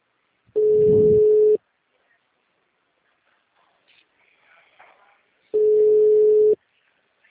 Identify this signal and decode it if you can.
Ringback tone (Europe, 425 Hz).